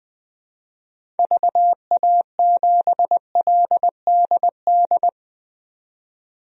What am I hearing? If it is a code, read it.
Morse code, heard as a keyed tone, about 20 wpm, 700 Hz. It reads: VA7LDD
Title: DXCC-100 Call Signs - 6 Characters - 20wpm